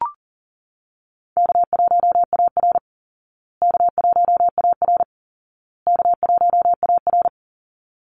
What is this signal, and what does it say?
Morse code, heard as a keyed tone, about 40 wpm, 700 Hz. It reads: K1AR K1AR K1AR
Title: Easy Call Signs Repeated 3x - 40wpm